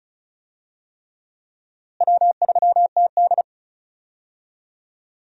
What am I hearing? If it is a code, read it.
Morse code, heard as a keyed tone, about 34 wpm, 700 Hz. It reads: W3TD
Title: USA Call Signs - 4 Characters - Repeated 3x - 35wpm